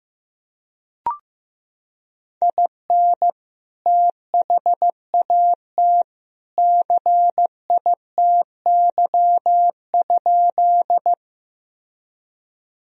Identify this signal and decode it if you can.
Morse code, heard as a keyed tone, about 16 wpm, 700 Hz. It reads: IN THAT CITY?
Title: Sentences from Top 1000 Words - Review Part 1 - 15wpm